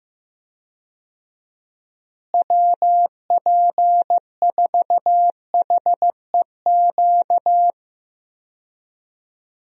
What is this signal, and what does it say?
Morse code, heard as a keyed tone, about 16 wpm, 700 Hz. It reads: WP4HEQ
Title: USA Call Signs - 6 Characters - Repeated 2x - 15wpm